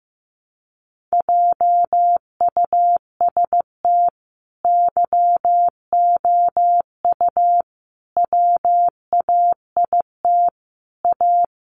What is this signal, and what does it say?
Morse code, heard as a keyed tone, about 15 wpm, 700 Hz. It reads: JUST YOU WAIT A
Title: Sentences from Top 900 Words 15wpm